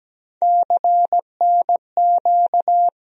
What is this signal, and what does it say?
Morse code, heard as a keyed tone, about 17 wpm, 700 Hz. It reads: CNQ
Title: USA Call Sign Postfixes - 3 Letters 17wpm